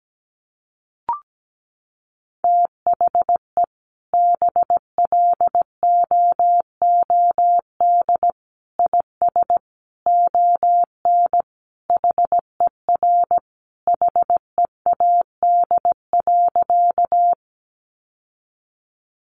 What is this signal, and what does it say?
Morse code, heard as a keyed tone, about 17 wpm, 700 Hz. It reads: THE BLOOD IS ON HER HEAD.
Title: Sentences from Top 1100 Words 17wpm